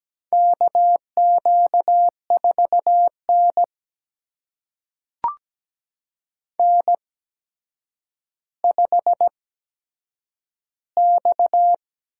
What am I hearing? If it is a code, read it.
Morse code, heard as a keyed tone, about 17 wpm, spaced as for 8 wpm, 700 Hz. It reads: KQ4N N5X
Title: USA Call Signs - 4 Characters - 8x Character Spacing - 17wpm